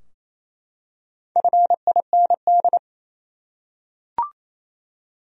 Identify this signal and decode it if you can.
Morse code, heard as a keyed tone, about 28 wpm, 700 Hz. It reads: FIND
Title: Top 100 Words 28wpm